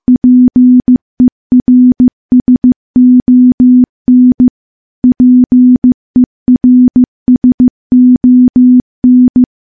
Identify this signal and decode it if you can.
Morse code, heard as a keyed tone, about 15 wpm, 250 Hz. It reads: PERSON PERSON
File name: morse / morse-training-15WPM